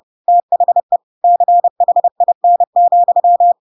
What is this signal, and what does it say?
Morse code, heard as a keyed tone, about 30 wpm, 700 Hz. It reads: THE CHIN,
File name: Pinocchio / Pinocchio ch3 30 30 ik1hge